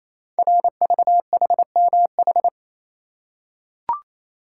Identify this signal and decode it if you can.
Morse code, heard as a keyed tone, about 27 wpm, 700 Hz. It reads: RVHMH
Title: Code Groups - 5 Letters - 28wpm